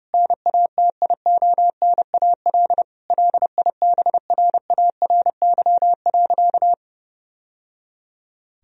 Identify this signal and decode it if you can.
Morse code, heard as a keyed tone, about 32 wpm, 700 Hz. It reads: NATIONAL LIBRARY.
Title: Sentences from Top 1000 Words - Review Part 4 - 30wpm